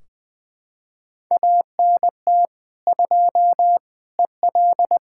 Morse code, heard as a keyed tone, about 20 wpm, 700 Hz. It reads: ANT 2 EL
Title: Practice QSO - Exchange 2 - Info - 20wpm